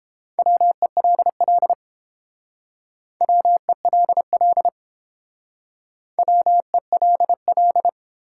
Morse code, heard as a keyed tone, about 29 wpm, 700 Hz. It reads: WELL WELL WELL